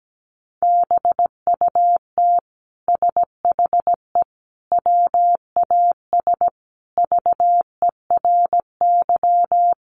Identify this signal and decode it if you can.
Morse code, heard as a keyed tone, about 17 wpm, 700 Hz. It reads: BUT SHE WAS VERY
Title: Sentences from Top 1000 Words 17wpm